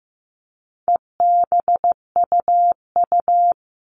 Morse code, heard as a keyed tone, about 15 wpm, 700 Hz. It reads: EBUU
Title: All Cut Numbers - 4-Digit 15wpm